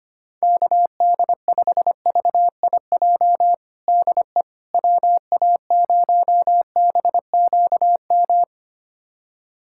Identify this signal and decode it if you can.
Morse code, heard as a keyed tone, about 25 wpm, 700 Hz. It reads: KD5VIJ DE WA0BQM
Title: USA Callsign DE Callsign - 6 Characters - 25wpm